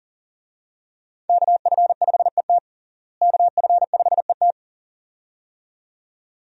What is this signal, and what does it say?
Morse code, heard as a keyed tone, about 40 wpm, 700 Hz. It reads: KF5ET KF5ET
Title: USA Call Signs - 5 Characters - Repeated 2x - 40wpm